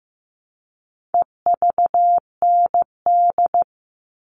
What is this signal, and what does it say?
Morse code, heard as a keyed tone, about 15 wpm, 700 Hz. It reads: EVND
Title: All Cut Numbers - 4-Digit 15wpm